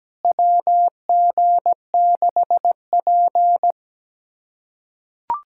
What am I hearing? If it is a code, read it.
Morse code, heard as a keyed tone, about 17 wpm, 700 Hz. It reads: WG6P
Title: USA Callsign DE Callsign - 4 Characters - 17wpm